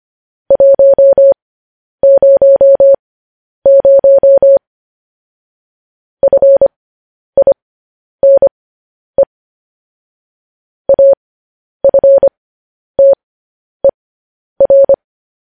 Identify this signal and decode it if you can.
Morse code, heard as a keyed tone, about 25 wpm, spaced as for 10 wpm, 550 Hz. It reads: 100 FINE AFTER
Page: news headlines in morse code at 10 wpm